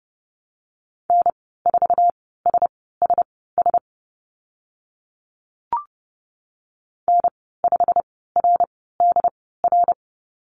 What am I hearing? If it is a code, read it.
Morse code, heard as a keyed tone, about 30 wpm, spaced as for 17 wpm, 700 Hz. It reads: N4SSS N5RDR
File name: Lesson 15g - Number 5 - Call Signs 30wpm